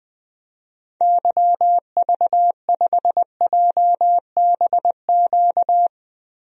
Morse code, heard as a keyed tone, about 21 wpm, 700 Hz. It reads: YV5JBQ
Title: DXCC-100 Call Signs - 6 Characters - Repeated 2x - 20wpm